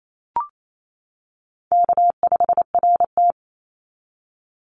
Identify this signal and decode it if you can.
Morse code, heard as a keyed tone, about 28 wpm, 700 Hz. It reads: K5RT
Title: Call Signs - 28wpm